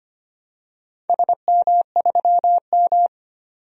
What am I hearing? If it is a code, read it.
Morse code, heard as a keyed tone, about 25 wpm, 700 Hz. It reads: SM3M
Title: DXCC-100 Call Signs - 4 Characters - 25wpm